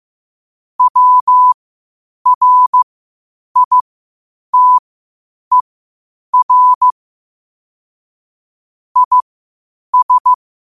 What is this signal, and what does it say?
Morse code, heard as a keyed tone, about 15 wpm, spaced as for 8 wpm, 1000 Hz. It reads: WRITER IS
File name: qotdfiles / 8wpm